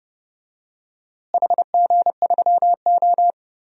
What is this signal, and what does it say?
Morse code, heard as a keyed tone, about 31 wpm, 700 Hz. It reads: HG3O
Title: DXCC-100 Call Signs - 4 Characters - Repeated 3x - 30wpm